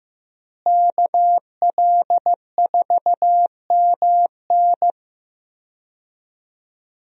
Morse code, heard as a keyed tone, about 15 wpm, 700 Hz. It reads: KL4MN
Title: USA Call Signs - 5 Characters - Repeated 2x - 15wpm